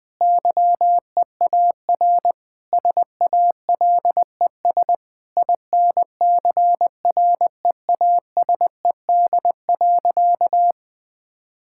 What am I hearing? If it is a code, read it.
Morse code, heard as a keyed tone, about 21 wpm, 700 Hz. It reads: YEAR SALES INCREASED.
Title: Sentences with 6-Word Phrases - 20wpm